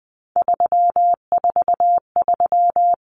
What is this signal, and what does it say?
Morse code, heard as a keyed tone, about 20 wpm, 700 Hz. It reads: 343
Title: Common Cut Numbers - 3-Digit 20wpm